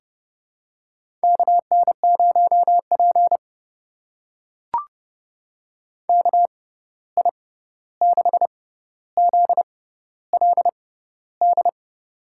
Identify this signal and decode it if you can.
Morse code, heard as a keyed tone, about 30 wpm, spaced as for 14 wpm, 700 Hz. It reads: KN0P KI6ZLD
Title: USA Call Signs - 6x Character Spacing - 30wpm